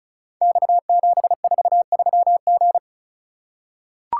Morse code, heard as a keyed tone, about 34 wpm, 690 Hz. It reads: X743G
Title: Code Groups - 5 Letter:Numbers - 35wpm